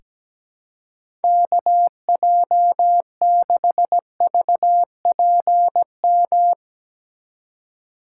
Morse code, heard as a keyed tone, about 17 wpm, 700 Hz. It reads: KJ6VPM
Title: USA Call Signs - 6 Characters - 2x Character Spacing - 17wpm